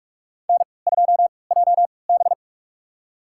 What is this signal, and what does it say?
Morse code, heard as a keyed tone, about 46 wpm, spaced as for 26 wpm, 700 Hz. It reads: N2JB
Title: USA Call Signs - 4 Characters - 3x Character Spacing - 45wpm